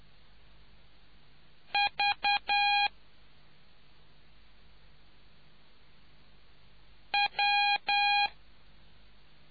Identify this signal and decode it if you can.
Morse code, heard as a keyed tone, about 10 wpm, 820 Hz. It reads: V W